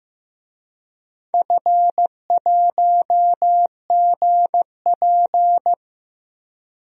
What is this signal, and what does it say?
Morse code, heard as a keyed tone, about 15 wpm, 700 Hz. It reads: F1GP
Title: DXCC-100 Call Signs - 4 Characters - 15wpm